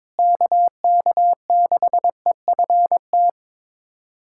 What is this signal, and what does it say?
Morse code, heard as a keyed tone, about 22 wpm, 700 Hz. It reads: KK6EFT
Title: USA Callsign DE Callsign - 6 Characters - 22wpm